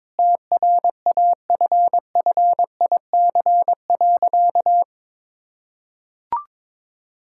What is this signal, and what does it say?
Morse code, heard as a keyed tone, about 22 wpm, 700 Hz. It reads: TRAFFIC.
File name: practice / 22wpm / Sentences with 8-Word Phrases - 22wpm